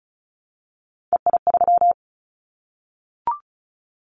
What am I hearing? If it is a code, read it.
Morse code, heard as a keyed tone, about 36 wpm, 690 Hz. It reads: EI3
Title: Callsign Prefixes for Top 50 DXCC - 35wpm